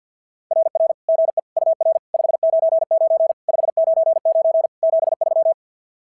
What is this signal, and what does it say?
Morse code, heard as a keyed tone, about 44 wpm, 660 Hz. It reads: RR GE UR 599 599 73
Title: SOTA QSO Practice 50wpm